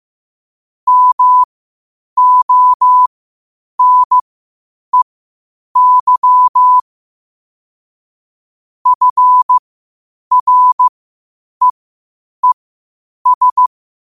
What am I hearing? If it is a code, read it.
Morse code, heard as a keyed tone, about 15 wpm, spaced as for 8 wpm, 1000 Hz. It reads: MONEY FREES
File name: qotdfiles / 8wpm